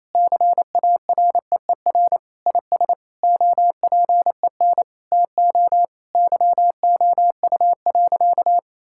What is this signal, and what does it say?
Morse code, heard as a keyed tone, about 28 wpm, 700 Hz. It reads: CAREER IS OPEN TO YOU.
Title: Sentences from Top 1000 Words - Review Part 6 - 28wpm